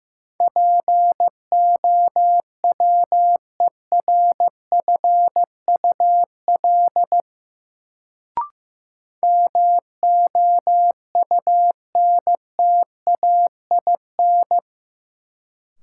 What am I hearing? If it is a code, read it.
Morse code, heard as a keyed tone, about 15 wpm, 700 Hz. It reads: POWERFUL MOUNTAIN
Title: Sentences from Top 1300 Words - New Word Drill Alt - 15wpm